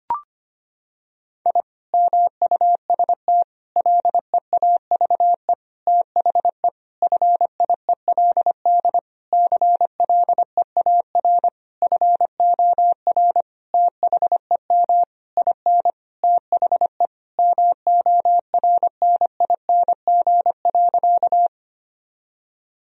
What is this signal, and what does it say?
Morse code, heard as a keyed tone, about 25 wpm, 700 Hz. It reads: I MUST LEAVE THE FIELD CLEAR FOR THEM IN THE MORNING.
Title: Sentences from Top 1000 Words - Review Part 2 - 25wpm